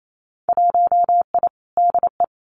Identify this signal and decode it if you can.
Morse code, heard as a keyed tone, about 28 wpm, 700 Hz. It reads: 1I DE
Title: Winter Field Day Exchanges Encore 28wpm